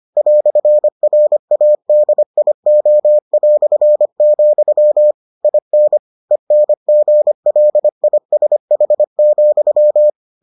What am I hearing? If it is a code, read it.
Morse code, heard as a keyed tone, about 25 wpm, 600 Hz. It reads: "RADIO", IN ENGLISH,